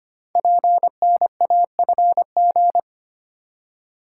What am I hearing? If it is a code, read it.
Morse code, heard as a keyed tone, about 25 wpm, 700 Hz. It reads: PNAFG
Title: Code Groups - 5 Letters - 25wpm